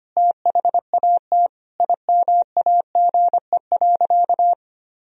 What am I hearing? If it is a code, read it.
Morse code, heard as a keyed tone, about 25 wpm, 700 Hz. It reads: THAT IMAGE.
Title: Sentences from Top 1000 Words - Review Part 4 - 25wpm